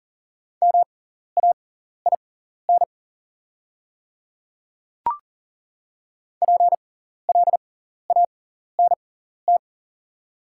Morse code, heard as a keyed tone, about 40 wpm, spaced as for 14 wpm, 700 Hz. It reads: MAIN PLANT
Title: 6x Character Spacing - Top 300 Words - 40wpm